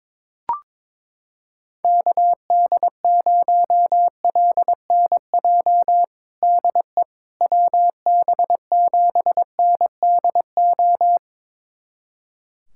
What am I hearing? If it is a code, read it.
Morse code, heard as a keyed tone, about 22 wpm, 700 Hz. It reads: KD0LNJ DE WB7NDO
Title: USA Callsign DE Callsign 22wpm